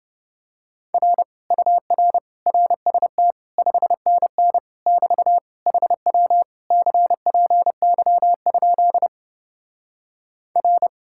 Morse code, heard as a keyed tone, about 28 wpm, 700 Hz. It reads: R UR RST 5NN <BT> HW CPY? R